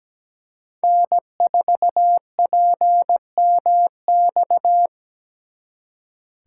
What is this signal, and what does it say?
Morse code, heard as a keyed tone, about 20 wpm, 700 Hz. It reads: N4PMX
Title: Speed Racing - USA Call Signs 17wpm